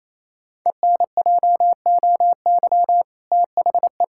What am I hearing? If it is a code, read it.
Morse code, heard as a keyed tone, about 28 wpm, 700 Hz. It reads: ENJOY THE
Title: Sentences from Top 1000 Words - Review Part 5 - 28wpm